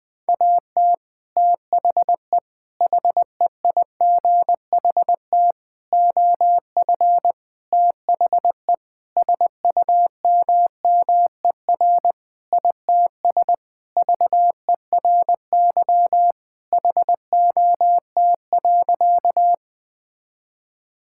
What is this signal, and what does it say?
Morse code, heard as a keyed tone, about 21 wpm, 700 Hz. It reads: AT THE HEIGHT OF THE SUMMER ITS VERY HOT.
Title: Sentences with 5-Word Phrases - Spoken Beginning - 20wpm